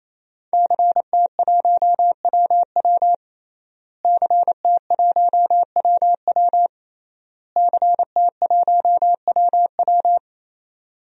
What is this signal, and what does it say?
Morse code, heard as a keyed tone, about 28 wpm, 700 Hz. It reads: CT1WW CT1WW CT1WW
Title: DXCC-100 Call Signs - 5 Characters - Repeated 3x - 28wpm